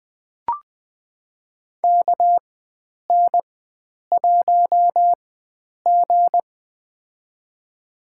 Morse code, heard as a keyed tone, about 20 wpm, spaced as for 9 wpm, 700 Hz. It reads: KN1G
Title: USA Call Signs - 4 Characters - 4x Character Spacing - 20wpm